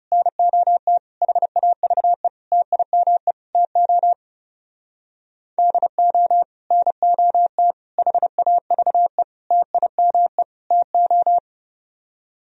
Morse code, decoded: NOT HAVE TIME TO DO NOT HAVE TIME TO

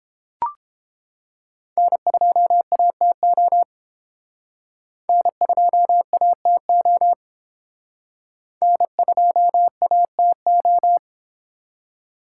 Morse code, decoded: N2ATO N2ATO N2ATO

